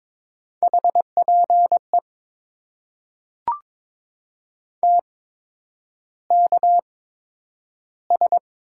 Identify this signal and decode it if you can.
Morse code, heard as a keyed tone, about 22 wpm, spaced as for 9 wpm, 700 Hz. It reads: HPE TKS
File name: Top 90 QSO Elements - 8x Character Spacing - 22wpm